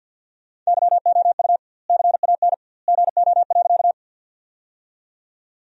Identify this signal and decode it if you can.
Morse code, heard as a keyed tone, about 50 wpm, 700 Hz. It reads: YOU CAN GO.
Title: 2x Word-Spacing - Sentences from Top 100 Words - 50wpm